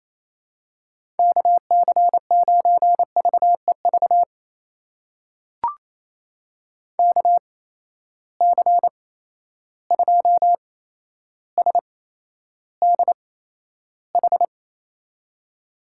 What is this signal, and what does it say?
Morse code, heard as a keyed tone, about 28 wpm, spaced as for 12 wpm, 700 Hz. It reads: KC9VEV KC2SDH
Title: USA Call Signs - 6 Characters - 8x Character Spacing - 28wpm